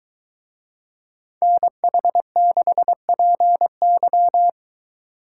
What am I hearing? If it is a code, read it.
Morse code, heard as a keyed tone, about 23 wpm, 700 Hz. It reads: NH6PY